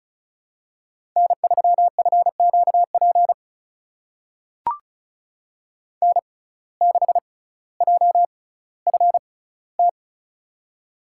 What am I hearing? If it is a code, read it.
Morse code, heard as a keyed tone, about 36 wpm, spaced as for 18 wpm, 700 Hz. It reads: N3FQP N6JFT
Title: USA Call Signs - 5 Characters - 6x Character Spacing - 35wpm